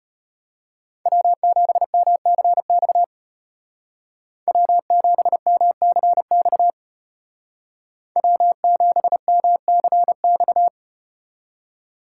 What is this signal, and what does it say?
Morse code, heard as a keyed tone, about 34 wpm, 700 Hz. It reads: W7MCX W7MCX W7MCX